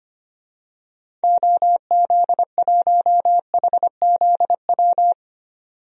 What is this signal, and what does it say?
Morse code, heard as a keyed tone, about 25 wpm, 700 Hz. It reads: OZ1HZW
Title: DXCC-100 Call Signs - 6 Characters - 25wpm